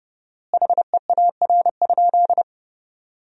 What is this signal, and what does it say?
Morse code, heard as a keyed tone, about 30 wpm, 700 Hz. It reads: HEAR?
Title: 4x Word-Spacing - Sentences from Top 300 Words - 30wpm